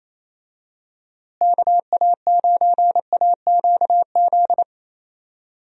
Morse code, heard as a keyed tone, about 28 wpm, 700 Hz. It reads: KA9AQZ
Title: USA Call Signs - 6 Characters - Repeated 3x - 28wpm